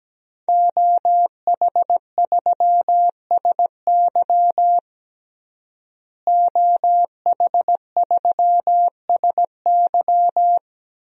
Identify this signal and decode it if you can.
Morse code, heard as a keyed tone, about 17 wpm, 700 Hz. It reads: OH3SY OH3SY
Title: DXCC-100 Call Signs - 5 Characters - Repeated 2x - 17wpm